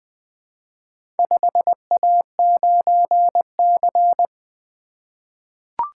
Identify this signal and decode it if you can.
Morse code, heard as a keyed tone, about 21 wpm, 700 Hz. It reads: 5A9C
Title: Four Letter-Number Combinations - 20wpm